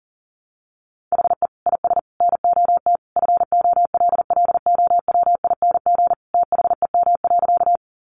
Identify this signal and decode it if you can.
Morse code, heard as a keyed tone, about 40 wpm, 700 Hz. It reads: HE IS NOT FOLLOWING THEM.
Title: Sentences from Top 700 Words 40wpm